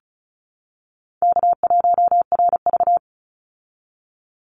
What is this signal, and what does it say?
Morse code, heard as a keyed tone, about 34 wpm, 700 Hz. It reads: K1RV
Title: Easy Call Signs Repeated 2x - 35wpm